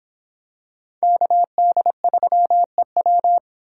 Morse code, heard as a keyed tone, about 25 wpm, 700 Hz. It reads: KD3EW